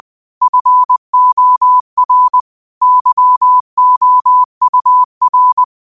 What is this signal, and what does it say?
Morse code, heard as a keyed tone, about 20 wpm, 1000 Hz. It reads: FOR YOUR
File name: qotdfiles / 20wpm